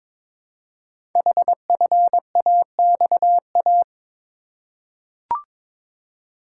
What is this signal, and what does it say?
Morse code, heard as a keyed tone, about 22 wpm, 700 Hz. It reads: HFAXA
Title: Code Groups - 5 Letters - 22wpm